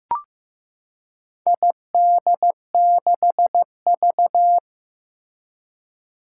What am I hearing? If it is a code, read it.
Morse code, heard as a keyed tone, about 15 wpm, 700 Hz. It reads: ID6V
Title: Four Challenging Letter-Number Combinations 15wpm